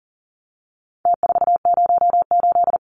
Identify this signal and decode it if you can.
Morse code, heard as a keyed tone, about 40 wpm, 700 Hz. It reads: T408
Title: Numbers - Contest Style - 4-Digit 40wpm